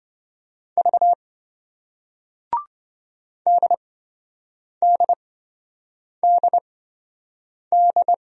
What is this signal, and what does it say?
Morse code, heard as a keyed tone, about 28 wpm, spaced as for 7 wpm, 700 Hz. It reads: V DDDD